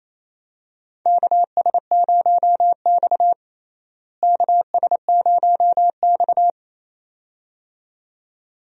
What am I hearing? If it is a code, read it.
Morse code, heard as a keyed tone, about 28 wpm, 700 Hz. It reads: KS0X KS0X
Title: USA Call Signs - 4 Characters - Repeated 2x - 28wpm